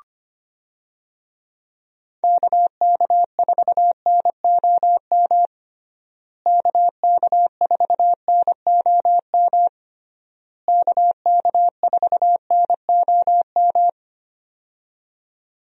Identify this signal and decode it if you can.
Morse code, heard as a keyed tone, about 25 wpm, 700 Hz. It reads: KK4NOM KK4NOM KK4NOM
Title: USA Call Signs - 6 Characters - Repeated 3x - 25wpm